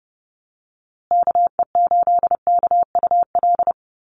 Morse code, heard as a keyed tone, about 30 wpm, 700 Hz. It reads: KE8KUL
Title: USA Call Signs Repeated 3x 30wpm